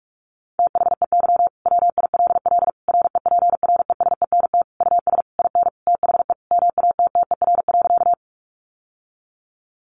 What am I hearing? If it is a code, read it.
Morse code, heard as a keyed tone, about 44 wpm, 700 Hz. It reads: THEY WILL REPRESENT US IN THE MATTER.
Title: Sentences from Top 1200 Words 45wpm